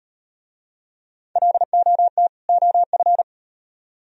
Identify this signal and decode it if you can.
Morse code, heard as a keyed tone, about 38 wpm, 700 Hz. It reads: LOT OF